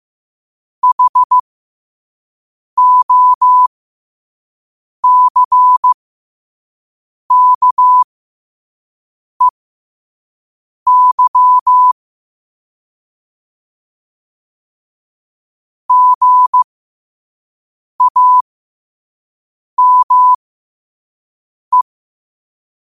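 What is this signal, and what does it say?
Morse code, heard as a keyed tone, about 15 wpm, spaced as for 5 wpm, 1000 Hz. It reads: HOCKEY GAME
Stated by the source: Morse code sent at 5 wpm.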